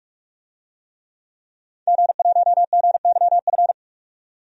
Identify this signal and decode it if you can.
Morse code, heard as a keyed tone, about 42 wpm, 700 Hz. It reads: G1GYF
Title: DXCC-100 Call Signs - 5 Characters - 45wpm